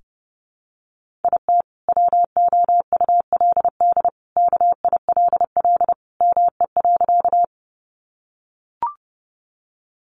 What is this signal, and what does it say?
Morse code, heard as a keyed tone, about 28 wpm, 700 Hz. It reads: IT WOULD KILL ME.
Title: Sentences from Top 1400 Words 30wpm